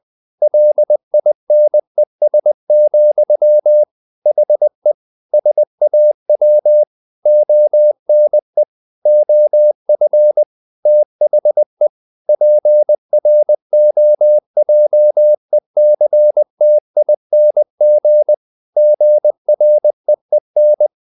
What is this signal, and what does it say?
Morse code, heard as a keyed tone, about 20 wpm, 600 Hz. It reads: LINES, HE SAW ONE OF THE PROJECTING GREEN
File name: the old man and the sea / the old man and the sea part 1 20wpm